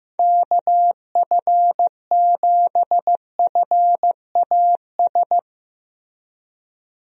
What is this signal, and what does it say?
Morse code, heard as a keyed tone, about 16 wpm, 700 Hz. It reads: KF7FAS
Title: USA Call Signs - 6 Characters - Repeated 2x - 15wpm